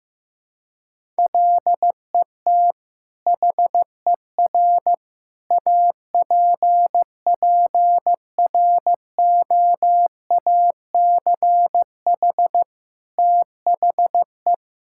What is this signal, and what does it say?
Morse code, heard as a keyed tone, about 16 wpm, 700 Hz. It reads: LET HER APPROACH THE
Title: Sentences from Top 1000 Words - Review Part 6 - 15wpm